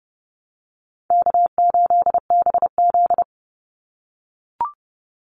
Morse code, heard as a keyed tone, about 30 wpm, 700 Hz. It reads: K8BZ
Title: Easy Call Signs - 30wpm